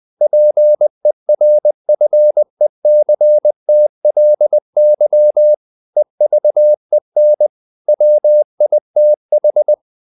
Morse code, heard as a keyed tone, about 20 wpm, 600 Hz. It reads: PERFECTLY EVEN WITH
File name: the old man and the sea / the old man and the sea part 1 20wpm